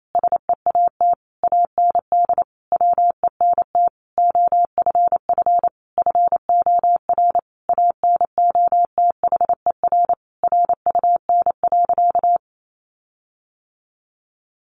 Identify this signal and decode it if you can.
Morse code, heard as a keyed tone, about 28 wpm, 700 Hz. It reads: SEAT AND WENT OFF FOR ANOTHER RUN.
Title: Sentences from Top 1500 Words 28wpm